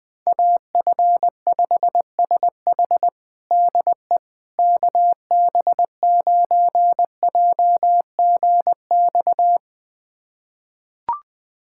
Morse code, heard as a keyed tone, about 20 wpm, 700 Hz. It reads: AF5SH DE KB9JGX
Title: USA Callsign DE Callsign 20wpm